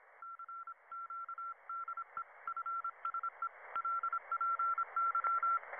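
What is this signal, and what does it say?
Morse code, heard as a keyed tone, about 26 wpm, 1360 Hz. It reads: CQ DE FSE LPY
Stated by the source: Ham radio contact in a fictional audio drama, After Disaster.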